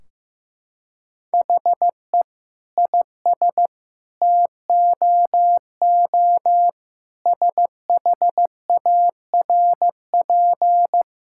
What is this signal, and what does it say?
Morse code, heard as a keyed tone, about 15 wpm, 700 Hz. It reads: HE IS TOO SHARP
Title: Sentences from Top 1600 Words 15wpm